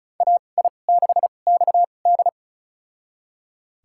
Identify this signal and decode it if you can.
Morse code, heard as a keyed tone, about 34 wpm, spaced as for 25 wpm, 700 Hz. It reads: AI6XD
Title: USA Call Signs - 5 Characters - 2x Character Spacing - 35wpm